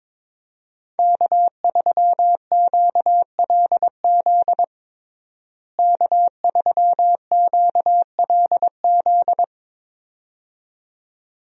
Morse code, heard as a keyed tone, about 22 wpm, 700 Hz. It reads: K3QLZ K3QLZ
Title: USA Call Signs - 5 Characters - Repeated 2x - 22wpm